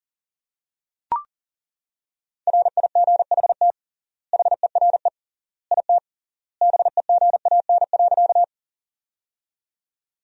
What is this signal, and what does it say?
Morse code, heard as a keyed tone, about 40 wpm, 700 Hz. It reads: RIGHT HERE IT BEGAN.
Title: 3x Word-Spacing - Sentences from Top 300 Words - 40wpm